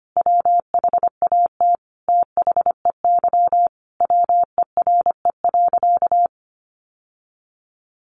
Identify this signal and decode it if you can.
Morse code, heard as a keyed tone, about 25 wpm, 700 Hz. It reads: WHAT THEY WERE.
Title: Sentences from Top 1000 Words 25wpm